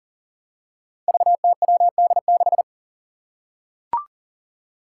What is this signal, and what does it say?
Morse code, heard as a keyed tone, about 40 wpm, 700 Hz. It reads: VTWD6
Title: Five Challenging Letter-Number Combinations 40wpm